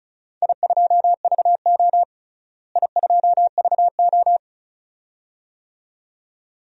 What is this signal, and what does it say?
Morse code, heard as a keyed tone, about 36 wpm, 690 Hz. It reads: I2VO I2VO
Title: DXCC-100 Call Signs - 4 Characters - Repeated 2x - 35wpm